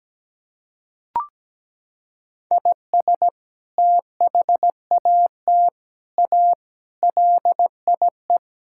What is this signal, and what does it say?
Morse code, heard as a keyed tone, about 17 wpm, 700 Hz. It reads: IS THAT A LIE
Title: Sentences from Top 1600 Words 17wpm